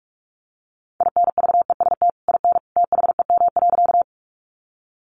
Morse code, heard as a keyed tone, about 44 wpm, 700 Hz. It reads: INVEST IN THEM.
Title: Sentences from Top 1100 Words 45wpm